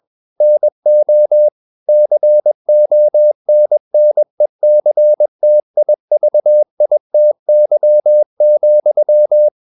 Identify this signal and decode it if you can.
Morse code, heard as a keyed tone, about 21 wpm, 600 Hz. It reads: NO CONNECTIVITY,
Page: CW med Gnister